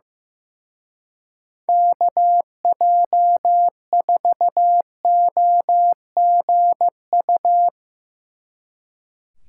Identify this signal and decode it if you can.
Morse code, heard as a keyed tone, about 16 wpm, 700 Hz. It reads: KJ4OGU
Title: Speed Racing - USA 2x3 Call Signs 15wpm